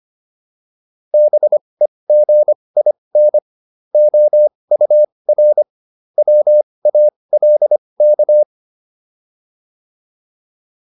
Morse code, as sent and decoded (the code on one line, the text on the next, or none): -... --. .. -. / --- ..- .-. / .-- .- .-.. -.-
BEGIN OUR WALK